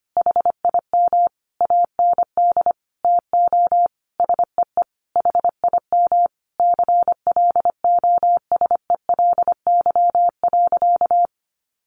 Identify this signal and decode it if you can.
Morse code, heard as a keyed tone, about 25 wpm, 700 Hz. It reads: HIM AND TO SEE HIM CLOSELY.
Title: Sentences from Top 1400 Words 25wpm